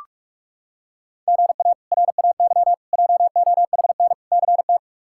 Morse code, heard as a keyed tone, about 45 wpm, 700 Hz. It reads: GA RAY JOHN CT
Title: K1USN SST QSO Practice - 45wpm